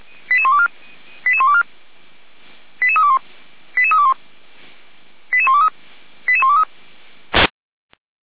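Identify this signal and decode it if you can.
Five-tone selective call (ZVEI), heard as quick runs of five tones, tones of about 70 ms. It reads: 88124 88124 88321 88321 88123 88123
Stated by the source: ZVEI five-tone selective call received by SDR, from the Feuerwehr Orlamünde (feuerwehr-orlamuende) audio server.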